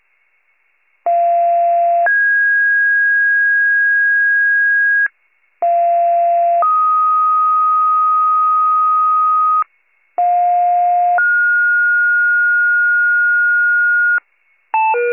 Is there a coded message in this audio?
Yes: two-tone page from 1.1 to 14.2 s, heard as a one-second tone, then a three-second tone, about 690 then 1650 Hz, about 690 then 1250 Hz, about 710 then 1430 Hz, with noise.